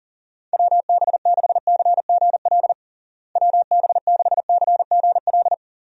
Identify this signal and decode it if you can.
Morse code, heard as a keyed tone, about 40 wpm, 700 Hz. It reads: WB6CGL WB6CGL
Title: USA Call Signs - 6 Characters - Repeated 2x - 40wpm